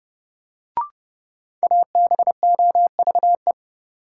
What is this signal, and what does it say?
Morse code, heard as a keyed tone, about 30 wpm, 700 Hz. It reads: ABOVE
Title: Sentences from Top 1000 Words - Review Part 2 - New Word Drill - 30wpm